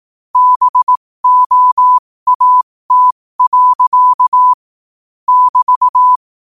Morse code, heard as a keyed tone, about 18 wpm, spaced as for 15 wpm, 1000 Hz. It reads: BOAT. <BT>
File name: qotdfiles / 15wpm